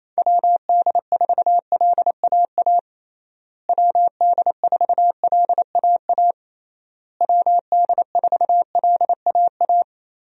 Morse code, heard as a keyed tone, about 28 wpm, 700 Hz. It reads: WD4LAA WD4LAA WD4LAA
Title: USA Call Signs - 6 Characters - Repeated 3x - 28wpm